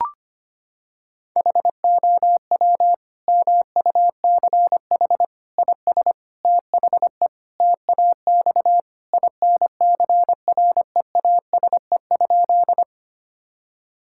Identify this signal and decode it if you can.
Morse code, heard as a keyed tone, about 25 wpm, 700 Hz. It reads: HOW MUCH IS THE TAX INCREASE?
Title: Sentences from Top 1000 Words - Review Part 5 - 25wpm